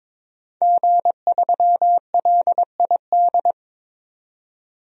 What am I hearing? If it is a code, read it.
Morse code, heard as a keyed tone, about 22 wpm, 700 Hz. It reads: G3LID
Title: DXCC-100 Call Signs - 5 Characters - 22wpm